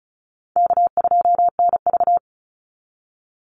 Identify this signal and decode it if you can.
Morse code, heard as a keyed tone, about 34 wpm, 700 Hz. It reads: K2NV
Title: Easy Call Signs Repeated 3x - 35wpm